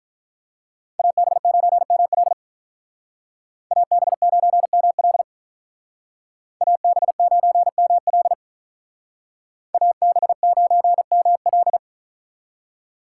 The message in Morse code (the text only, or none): AB9ML AB9ML AB9ML AB9ML